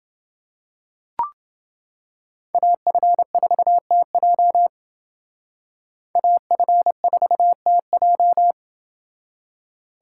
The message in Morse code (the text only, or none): AF4TJ AF4TJ